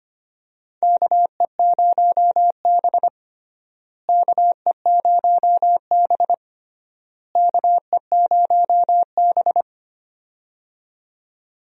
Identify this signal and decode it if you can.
Morse code, heard as a keyed tone, about 25 wpm, 700 Hz. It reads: KE0B KE0B KE0B
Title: USA Call Signs - 4 Characters - Repeated 3x - 25wpm